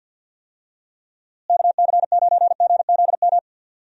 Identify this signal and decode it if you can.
Morse code, heard as a keyed tone, about 52 wpm, 700 Hz. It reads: KC9GZM
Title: USA Call Signs - 6 Characters - Repeated 3x - 50wpm